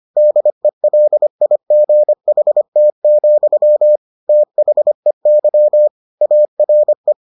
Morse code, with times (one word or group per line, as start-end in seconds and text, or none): 0.2-4.0 DELIGHT,
4.3-5.9 THEY
6.2-7.1 ARE